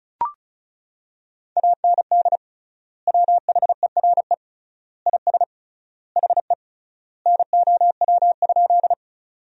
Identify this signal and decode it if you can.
Morse code, heard as a keyed tone, about 34 wpm, 700 Hz. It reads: AND WHERE IS HE NOW?
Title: 3x Word-Spacing - Sentences from Top 200 Words - 35wpm